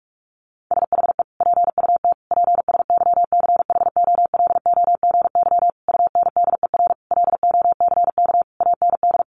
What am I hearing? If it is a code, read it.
Morse code, heard as a keyed tone, about 44 wpm, 700 Hz. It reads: SHE PUT PSYCHOLOGY UNDER LOCK AND
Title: Sentences from Top 1400 Words 45wpm